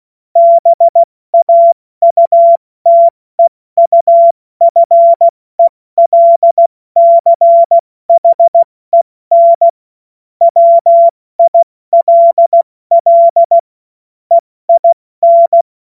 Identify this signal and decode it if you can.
Morse code, heard as a keyed tone, about 16 wpm, spaced as for 14 wpm, 680 Hz. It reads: BAUTEUFELCHEN WILL EIN
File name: Morsekurs / Witze / Witz 28.November 2023 70 F80